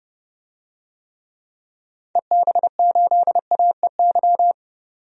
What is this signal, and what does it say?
Morse code, heard as a keyed tone, about 30 wpm, 700 Hz. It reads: EB8AEY
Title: DXCC-100 Call Signs - 6 Characters - 30wpm